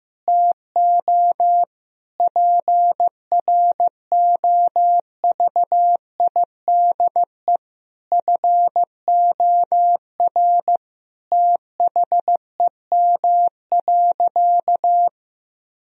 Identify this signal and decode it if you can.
Morse code, heard as a keyed tone, about 15 wpm, 700 Hz. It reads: TO PROVIDE FOR THEM.
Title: Sentences from Top 1000 Words - Review Part 3 - 15wpm